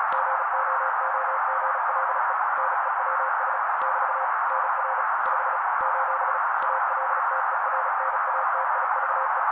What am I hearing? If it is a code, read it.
Morse code, heard as a keyed tone, about 35 wpm, 530 Hz. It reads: GOOGLE NEWS <BT> NR 5 : ALTERNATIVE
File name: daily / 35wpm noise